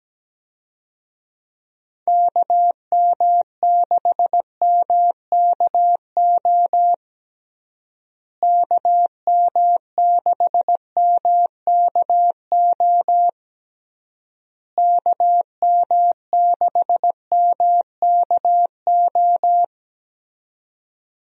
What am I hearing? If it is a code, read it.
Morse code, heard as a keyed tone, about 17 wpm, 700 Hz. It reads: KM6MKO KM6MKO KM6MKO
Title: USA Call Signs - 6 Characters - Repeated 3x - 17wpm